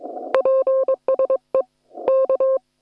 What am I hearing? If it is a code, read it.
Morse code, heard as a keyed tone, about 22 wpm, 530 Hz. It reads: PSE K